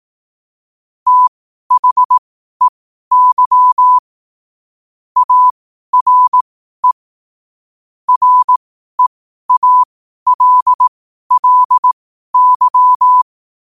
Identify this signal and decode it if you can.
Morse code, heard as a keyed tone, about 18 wpm, spaced as for 12 wpm, 1000 Hz. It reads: THEY ARE REALLY
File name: qotdfiles / 12wpm